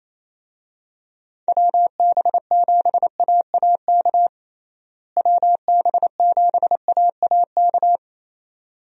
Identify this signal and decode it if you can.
Morse code, heard as a keyed tone, about 28 wpm, 700 Hz. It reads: WB7AAK WB7AAK
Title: USA Call Signs - 6 Characters - Repeated 2x - 28wpm